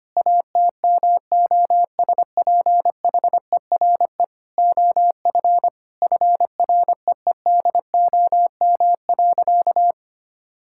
Morse code, heard as a keyed tone, about 25 wpm, 700 Hz. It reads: ATMOSPHERE OF FREEDOM.
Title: Sentences from Top 1600 Words 25wpm